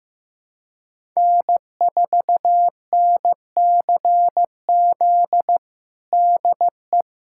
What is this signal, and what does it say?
Morse code, heard as a keyed tone, about 15 wpm, 700 Hz. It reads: N4NCZ DE
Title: USA Callsign DE Callsign - 5 Characters - 15wpm